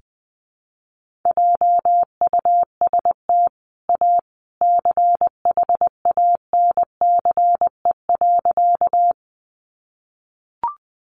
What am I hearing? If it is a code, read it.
Morse code, heard as a keyed tone, about 20 wpm, 700 Hz. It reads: JUST A CHANCE.
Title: Sentences from Top 800 Words 20wpm